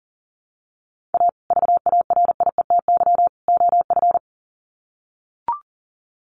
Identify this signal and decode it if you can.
Morse code, heard as a keyed tone, about 40 wpm, 700 Hz. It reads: A VARIETY OF
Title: Sets of 3 Words 40wpm